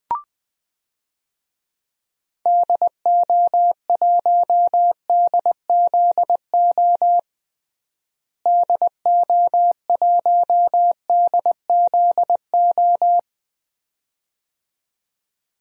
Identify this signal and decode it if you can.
Morse code, heard as a keyed tone, about 20 wpm, 700 Hz. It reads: DO1DZO DO1DZO
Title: DXCC-100 Call Signs - 6 Characters - Repeated 2x - 20wpm